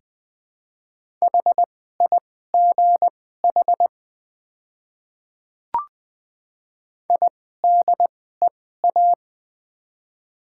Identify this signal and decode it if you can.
Morse code, heard as a keyed tone, about 20 wpm, spaced as for 14 wpm, 700 Hz. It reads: HIGH IDEA
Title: 2x Character Spacing - Top 300 Words - 20wpm